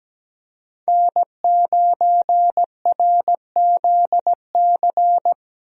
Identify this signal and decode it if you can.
Morse code, heard as a keyed tone, about 17 wpm, 700 Hz. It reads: N9RZC
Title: USA Call Signs - 5 Characters - 17wpm